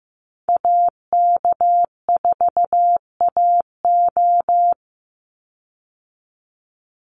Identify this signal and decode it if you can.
Morse code, heard as a keyed tone, about 15 wpm, 700 Hz. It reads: AK4AO
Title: Call Signs Repeated 3x - 15wpm